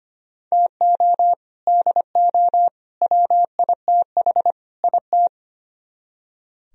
Morse code, heard as a keyed tone, about 33 wpm, 700 Hz. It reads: TO DO WITH IT